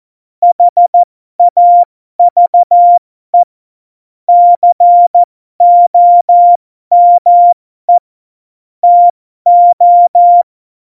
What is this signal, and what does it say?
Morse code, heard as a keyed tone, about 14 wpm, spaced as for 12 wpm, 700 Hz. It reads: HAVE COME TO
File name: Morsekurs / Witze